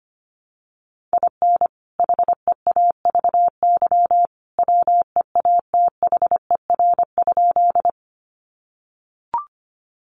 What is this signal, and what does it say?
Morse code, heard as a keyed tone, about 25 wpm, 700 Hz. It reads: IN HEAVY WEATHER?
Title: Sentences from Top 1100 Words 25wpm